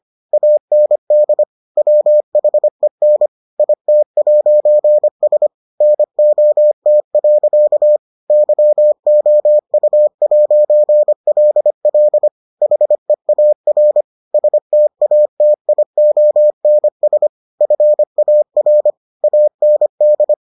Morse code, heard as a keyed tone, about 25 wpm, 600 Hz. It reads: AND WHEN IT'S NOT. YOU'LL HEAR STATIONS FAR AND